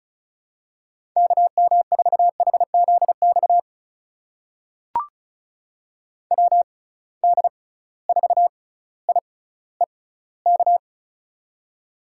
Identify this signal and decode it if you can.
Morse code, heard as a keyed tone, about 36 wpm, spaced as for 18 wpm, 700 Hz. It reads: KM4HZX WD4IEK
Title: USA Call Signs - 6 Characters - 6x Character Spacing - 35wpm